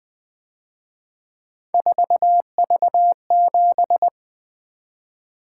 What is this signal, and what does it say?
Morse code, heard as a keyed tone, about 24 wpm, 700 Hz. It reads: 4V7
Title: Speed-Racing - Three Letter-Number Combinations - 20wpm